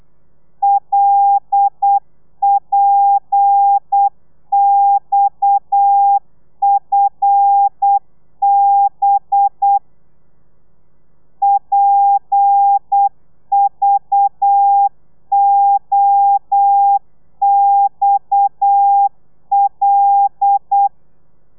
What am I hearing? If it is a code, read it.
Morse code, heard as a keyed tone, about 8 wpm, 800 Hz. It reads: LPXFB PVOXL